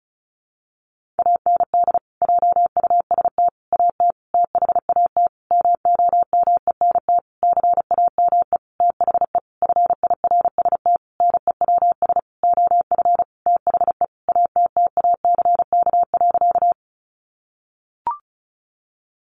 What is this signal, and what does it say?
Morse code, heard as a keyed tone, about 34 wpm, 700 Hz. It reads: AND JUST AT THAT MOMENT CAME THE FIRST NEWS OF THE ATTACK.
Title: Sentences from Top 1500 Words 35wpm